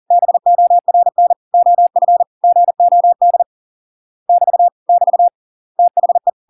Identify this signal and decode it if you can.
Morse code, heard as a keyed tone, about 40 wpm, 700 Hz. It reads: BORN OF GOD <BT> <BT> THE